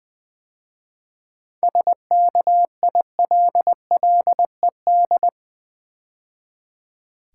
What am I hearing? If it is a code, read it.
Morse code, heard as a keyed tone, about 20 wpm, 700 Hz. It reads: SKILLED
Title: 7 Letter Words - First 3 Letters Given - 20wpm